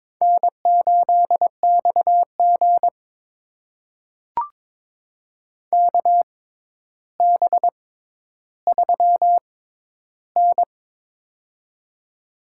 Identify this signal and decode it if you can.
Morse code, heard as a keyed tone, about 21 wpm, spaced as for 11 wpm, 700 Hz. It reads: N8XG KB3N
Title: USA Call Signs - 4 Characters - 6x Character Spacing - 22wpm